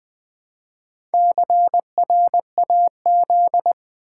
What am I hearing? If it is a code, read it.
Morse code, heard as a keyed tone, about 20 wpm, 700 Hz. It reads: CRAZ
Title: Build A Word - 1 Letter At A Time - 5-Letter Words - 20wpm